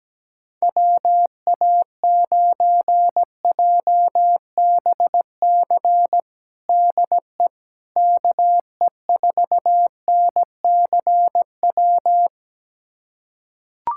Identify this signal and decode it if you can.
Morse code, heard as a keyed tone, about 17 wpm, 700 Hz. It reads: WA9JBC DE KE4NCW
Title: USA Callsign DE Callsign - 6 Characters - 17wpm